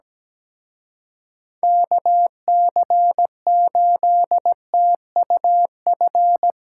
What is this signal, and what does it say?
Morse code, heard as a keyed tone, about 19 wpm, 700 Hz. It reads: KC8TUF
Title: Speed Racing - USA 2x3 Call Signs 17wpm